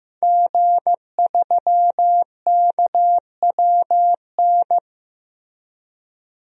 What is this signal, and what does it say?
Morse code, heard as a keyed tone, about 16 wpm, 700 Hz. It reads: G3KWN
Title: DXCC-100 Call Signs - 5 Characters - 15wpm